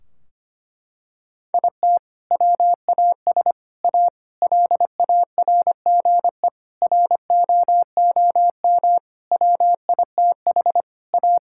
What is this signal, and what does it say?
Morse code, heard as a keyed tone, about 25 wpm, 700 Hz. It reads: IT WAS A LARGE ROOM WITH A
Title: Sentences from Top 1600 Words 25wpm